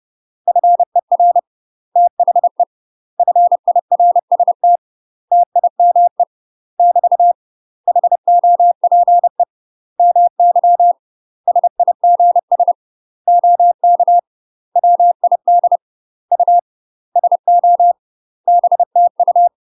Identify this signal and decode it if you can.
Morse code, heard as a keyed tone, about 30 wpm, 700 Hz. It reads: FER THE FIRST TIME <BT> HOPE MY SIGS OK WID U SO BTU
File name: UK MASTER QSOs TRAINING / 30WPM / UK MASTER QSOs 0018